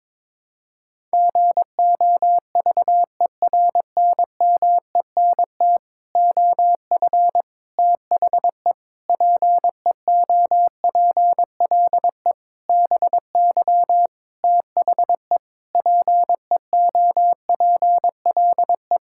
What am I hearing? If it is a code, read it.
Morse code, heard as a keyed tone, about 22 wpm, 700 Hz. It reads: GOVERNMENT OF THE PEOPLE BY THE PEOPLE
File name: Sets of 9 Words - 22wpm